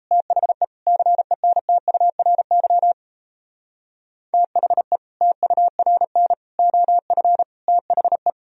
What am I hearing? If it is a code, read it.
Morse code, heard as a keyed tone, about 37 wpm, 700 Hz. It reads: THE CENTURY THE TURN OF THE